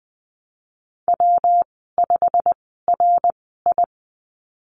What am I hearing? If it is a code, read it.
Morse code, heard as a keyed tone, about 20 wpm, spaced as for 15 wpm, 700 Hz. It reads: W5RI
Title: 193 Lesson 31g - Letter v - Call Signs 20wpm